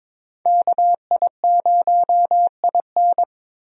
Morse code, heard as a keyed tone, about 22 wpm, 700 Hz. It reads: KI0IN
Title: USA Call Signs - 5 Characters - Repeated 2x - 22wpm